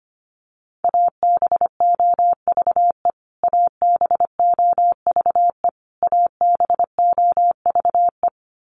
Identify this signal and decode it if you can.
Morse code, heard as a keyed tone, about 25 wpm, 700 Hz. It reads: ABOVE ABOVE ABOVE
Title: Top 500 Words Repeated 25wpm